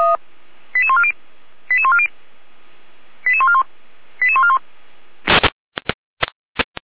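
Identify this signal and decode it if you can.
Five-tone selective call (ZVEI), heard as quick runs of five tones, tones of about 70 ms. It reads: 88140 88140 88141 88141
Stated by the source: ZVEI five-tone selective call 88149 received via SDR, from the Feuerwehr Orlamünde (feuerwehr-orlamuende.org) audio server.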